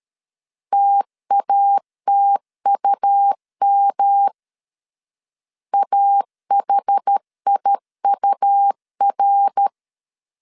Morse code, heard as a keyed tone, about 13 wpm, 800 Hz. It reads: TATUM AHIUR
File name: CW / DET / 1 3 1500C S C 12wpm 2750